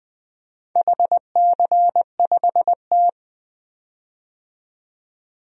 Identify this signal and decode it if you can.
Morse code, heard as a keyed tone, about 20 wpm, 700 Hz. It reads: HC5T
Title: DXCC-100 Call Signs - 4 Characters - 20wpm